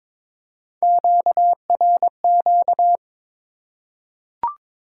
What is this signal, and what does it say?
Morse code, heard as a keyed tone, about 22 wpm, 700 Hz. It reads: QRQ
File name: Top 90 QSO Elements - 3x Character Spacing - 22wpm